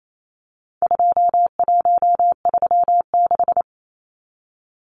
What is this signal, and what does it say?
Morse code, heard as a keyed tone, about 28 wpm, 700 Hz. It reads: 2136
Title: Numbers - 4-Digit 28wpm